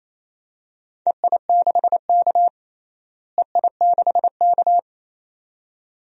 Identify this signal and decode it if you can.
Morse code, heard as a keyed tone, about 28 wpm, 700 Hz. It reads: EI6K EI6K
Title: DXCC-100 Call Signs - 4 Characters - Repeated 2x - 28wpm